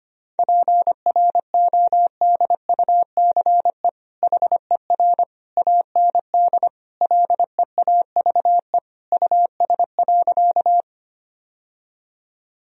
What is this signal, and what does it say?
Morse code, heard as a keyed tone, about 25 wpm, 700 Hz. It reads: PRODUCE HER AND LEAVE US.
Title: Sentences from Top 1000 Words - Review Part 3 - 25wpm